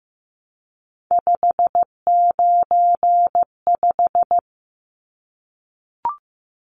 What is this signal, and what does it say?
Morse code, heard as a keyed tone, about 15 wpm, 700 Hz. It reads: 595
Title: Numbers - 3-Digit 15wpm